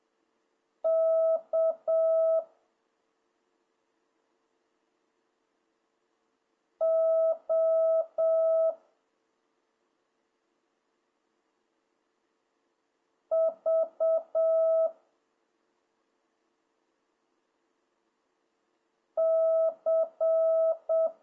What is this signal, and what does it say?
Morse code, heard as a keyed tone, about 7 wpm, 650 Hz. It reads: K O V C